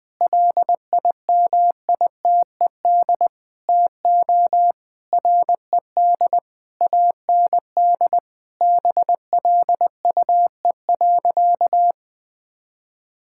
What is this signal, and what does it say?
Morse code, heard as a keyed tone, about 21 wpm, 700 Hz. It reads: LIMITED TO RED AND BLUE.
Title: Sentences with 6-Word Phrases - 20wpm